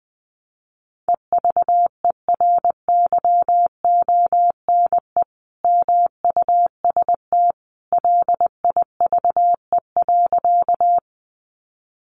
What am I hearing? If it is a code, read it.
Morse code, heard as a keyed tone, about 20 wpm, 700 Hz. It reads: EVERYONE MUST LIVE.
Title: Sentences from Top 800 Words 20wpm